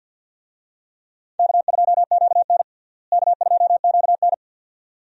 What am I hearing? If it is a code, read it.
Morse code, heard as a keyed tone, about 52 wpm, 700 Hz. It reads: K2QN K2QN
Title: USA Call Signs - 4 Characters - Repeated 2x - 50wpm